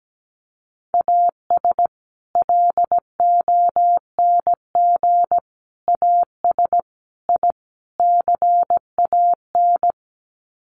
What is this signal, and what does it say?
Morse code, heard as a keyed tone, about 17 wpm, 700 Hz. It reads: AS LONG AS I CAN
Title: Sets of 5 Words Encore2 17wpm